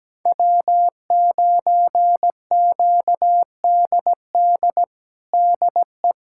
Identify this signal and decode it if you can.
Morse code, heard as a keyed tone, about 17 wpm, 700 Hz. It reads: W9QDD DE
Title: USA Callsign DE Callsign 17wpm